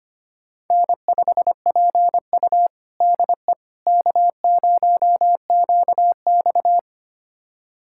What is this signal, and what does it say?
Morse code, heard as a keyed tone, about 25 wpm, 700 Hz. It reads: N5PU DE K0QX
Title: USA Callsign DE Callsign - 4 Characters - 25wpm